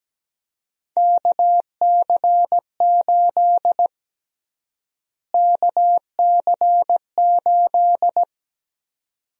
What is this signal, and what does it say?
Morse code, heard as a keyed tone, about 17 wpm, 700 Hz. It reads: KC8 KC8
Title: USA Call Sign Prefixes - Repeated 2x - 17wpm